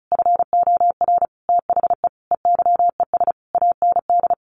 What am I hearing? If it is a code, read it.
Morse code, heard as a keyed tone, about 34 wpm, 700 Hz. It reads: FOR THE EYES AND
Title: Easy Phrases 35wpm